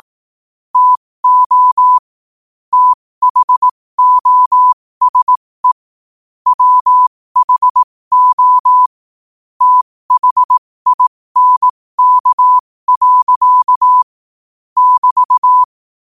Morse code, heard as a keyed tone, about 18 wpm, spaced as for 15 wpm, 1000 Hz. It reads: TO THOSE WHO THINK. <BT>
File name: qotdfiles / 15wpm